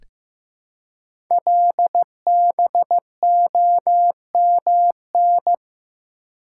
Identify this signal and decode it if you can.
Morse code, heard as a keyed tone, about 15 wpm, 700 Hz. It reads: LBOMN
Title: Code Groups - 5 Letters - 15wpm